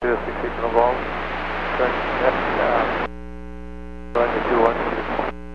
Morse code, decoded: none

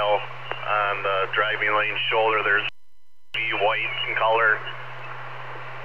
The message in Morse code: none

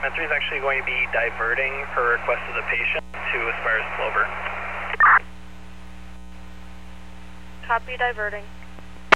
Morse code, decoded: none

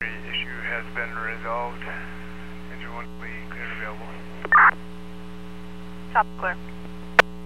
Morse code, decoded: none

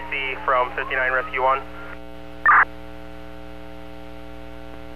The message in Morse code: none

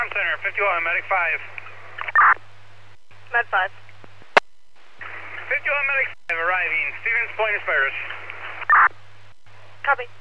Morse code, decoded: none

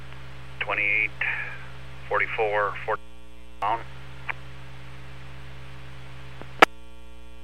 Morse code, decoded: none

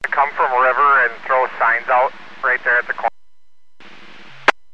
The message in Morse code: none

